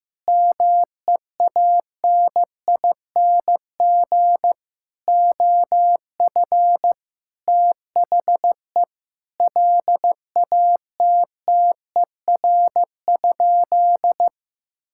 Morse code, heard as a keyed tone, about 16 wpm, 700 Hz. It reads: MEANING OF THE LATTER?
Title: Sentences from Top 1600 Words 15wpm